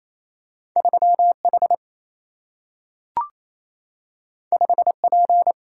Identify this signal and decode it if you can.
Morse code, heard as a keyed tone, about 28 wpm, 700 Hz. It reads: 3H 5P
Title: Single-Letter Single-Number 28wpm